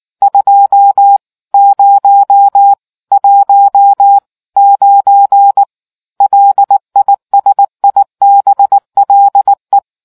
Morse code, heard as a keyed tone, about 19 wpm, 800 Hz. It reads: 2019 LISIBLE